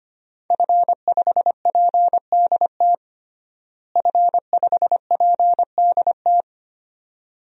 Morse code, heard as a keyed tone, about 25 wpm, 700 Hz. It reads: F5PDT F5PDT
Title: DXCC-100 Call Signs - 5 Characters - Repeated 2x - 25wpm